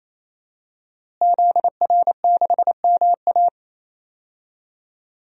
Morse code, heard as a keyed tone, about 28 wpm, 700 Hz. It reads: ZR6MA
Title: DXCC-100 Call Signs - 5 Characters - Repeated 3x - 28wpm